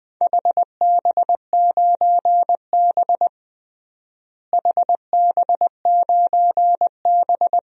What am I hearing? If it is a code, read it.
Morse code, heard as a keyed tone, about 20 wpm, 700 Hz. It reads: HB9B HB9B
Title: DXCC-100 Call Signs - 4 Characters - Repeated 2x - 20wpm